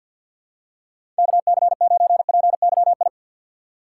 Morse code, heard as a keyed tone, about 51 wpm, 700 Hz. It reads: KC9PYI
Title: USA Call Signs - 6 Characters - 3x Character Spacing - 50wpm